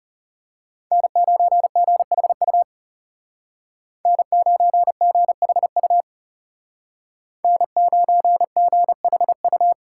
Morse code, heard as a keyed tone, about 35 wpm, 700 Hz. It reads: N9GHU N9GHU N9GHU
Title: Speed Racing - USA Call Signs 30wpm